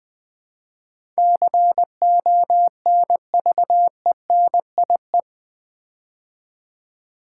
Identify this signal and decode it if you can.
Morse code, heard as a keyed tone, about 20 wpm, 700 Hz. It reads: CONVENIE
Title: Build A Word - 4 Letters At A Time - 10-Letter Words - 20wpm